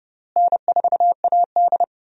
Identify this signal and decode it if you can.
Morse code, heard as a keyed tone, about 29 wpm, 700 Hz. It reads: N4AD
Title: USA Call Signs - 4 Characters - Repeated 3x - 30wpm